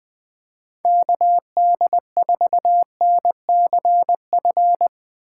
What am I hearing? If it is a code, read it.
Morse code, heard as a keyed tone, about 25 wpm, 700 Hz. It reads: KD4NCF
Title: Speed Racing - USA 2x3 Call Signs 20wpm